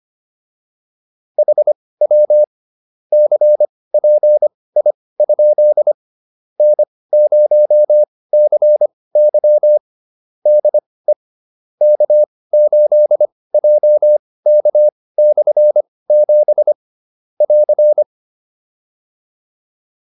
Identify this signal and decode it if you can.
Morse code, heard as a keyed tone, about 25 wpm, spaced as for 18 wpm, 600 Hz. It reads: HW CPI? N0CY DE K8JK/7 <AR>